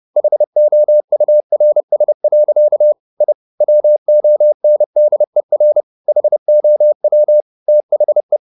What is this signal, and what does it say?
Morse code, heard as a keyed tone, about 30 wpm, 600 Hz. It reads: HOURS. I WONDER HOW THE
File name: the old man and the sea / the old man and the sea 1 30wpm